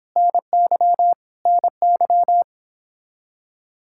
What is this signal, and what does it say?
Morse code, heard as a keyed tone, about 26 wpm, 700 Hz. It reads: NY NY
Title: Speed-Racing Common QSO Phrases - 22wpm